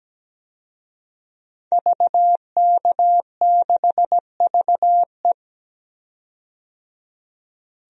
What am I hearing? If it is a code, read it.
Morse code, heard as a keyed tone, about 17 wpm, 700 Hz. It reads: VK6VE
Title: DXCC-100 Call Signs - 5 Characters - 17wpm